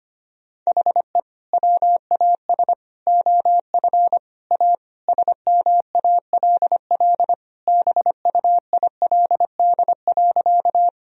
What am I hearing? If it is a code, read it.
Morse code, heard as a keyed tone, about 25 wpm, 700 Hz. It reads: HE WAS OF A SMALL BUILD.
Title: Sentences from Top 1000 Words - Review Part 1 - 25wpm